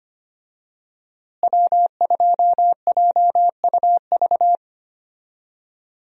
Morse code, heard as a keyed tone, about 25 wpm, 700 Hz. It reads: W2JUV
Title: USA Call Signs - 5 Characters - Repeated 2x - 25wpm